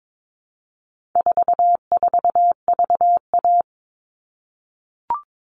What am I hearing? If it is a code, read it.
Morse code, heard as a keyed tone, about 22 wpm, 700 Hz. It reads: 44VA
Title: All Cut Numbers - 4-Digit 22wpm